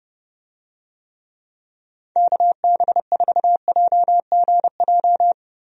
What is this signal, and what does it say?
Morse code, heard as a keyed tone, about 28 wpm, 700 Hz. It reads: KB4JGJ
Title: USA Call Signs - 6 Characters - Repeated 3x - 30wpm